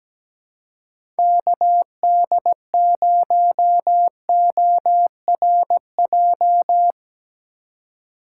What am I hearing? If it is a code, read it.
Morse code, heard as a keyed tone, about 17 wpm, 700 Hz. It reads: KD0ORJ
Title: USA Call Signs - 6 Characters - 2x Character Spacing - 17wpm